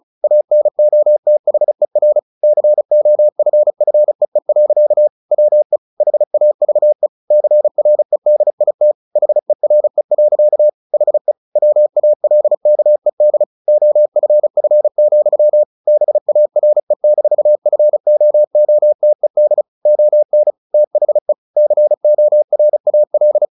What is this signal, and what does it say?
Morse code, heard as a keyed tone, about 34 wpm, 600 Hz. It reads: ANOTHER COFFEE. WE HAVE CREDIT HERE. HE WALKED OFF, BARE-FOOTED ON THE CORAL
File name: the old man and the sea / the old man and the sea 1 35wpm